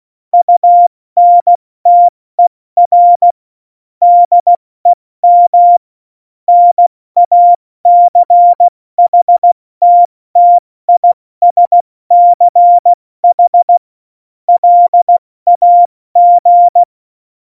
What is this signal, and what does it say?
Morse code, heard as a keyed tone, about 16 wpm, 700 Hz. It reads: UNTER DEM NACHTTISCH LAG
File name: Morsekurs / Witze / Witz 01. September 2020, 70 F80